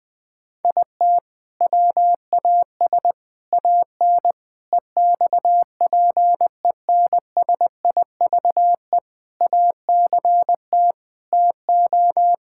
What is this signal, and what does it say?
Morse code, heard as a keyed tone, about 20 wpm, 700 Hz. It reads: IT WAS AN EXPENSIVE ACT TO
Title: Sentences from Top 1000 Words - Review Part 6 - 20wpm